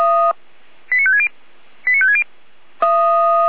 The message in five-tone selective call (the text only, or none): 89560 89560